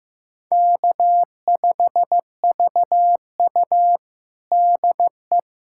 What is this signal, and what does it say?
Morse code, heard as a keyed tone, about 15 wpm, 700 Hz. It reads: K5VU DE
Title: USA Callsign DE Callsign - 4 Characters - 15wpm